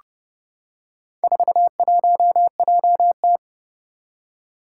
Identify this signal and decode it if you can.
Morse code, heard as a keyed tone, about 31 wpm, 700 Hz. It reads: 41JT
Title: Four Challenging Letter-Number Combinations 30wpm